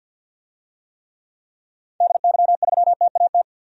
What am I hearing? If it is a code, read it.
Morse code, heard as a keyed tone, about 48 wpm, 700 Hz. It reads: DY3TAT
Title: DXCC-100 Call Signs - 6 Characters - 50wpm